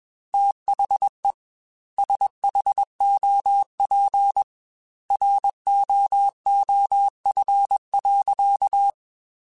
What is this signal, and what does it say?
Morse code, heard as a keyed tone, about 21 wpm, 800 Hz. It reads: THE SHOP ROOF.
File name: Weather Sets / Weather Sets-mp3s / Weather 4 28 @20